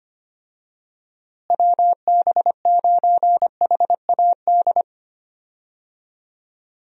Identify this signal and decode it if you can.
Morse code, heard as a keyed tone, about 25 wpm, 700 Hz. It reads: WB9HAD
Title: USA Call Signs - 6 Characters - Repeated 3x - 25wpm